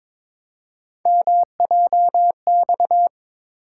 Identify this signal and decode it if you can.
Morse code, heard as a keyed tone, about 22 wpm, 700 Hz. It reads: MJX
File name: practice / 22wpm / Three Letter-Number Combinations - 22wpm